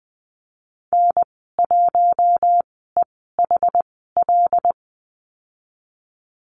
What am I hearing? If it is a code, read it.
Morse code, heard as a keyed tone, about 20 wpm, spaced as for 15 wpm, 700 Hz. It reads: N1EHL
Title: 082 Lesson 14g - Number 2 - Call Signs 20wpm